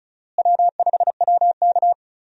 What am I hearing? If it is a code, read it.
Morse code, heard as a keyed tone, about 34 wpm, 700 Hz. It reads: W5WK
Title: USA Call Signs - 4 Characters - Repeated 2x - 35wpm